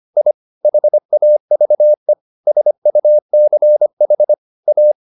Morse code, heard as a keyed tone, about 25 wpm, 600 Hz. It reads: I HAVE SUCH A
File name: the old man and the sea part 1 25wpm